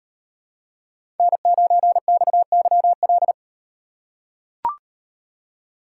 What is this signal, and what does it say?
Morse code, heard as a keyed tone, about 37 wpm, 700 Hz. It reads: N9XYL